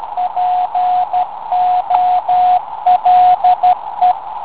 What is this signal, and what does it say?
Morse code, heard as a keyed tone, about 13 wpm, 730 Hz, with radio noise. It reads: POLE